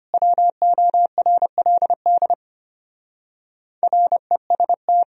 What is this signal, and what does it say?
Morse code, heard as a keyed tone, about 32 wpm, 700 Hz. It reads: WORLD REST